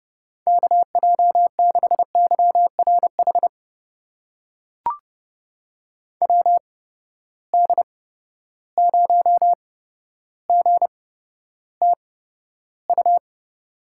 Morse code, heard as a keyed tone, about 30 wpm, spaced as for 13 wpm, 700 Hz. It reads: KJ6YRH WD0GTU
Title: USA Call Signs - 6 Characters - 8x Character Spacing - 30wpm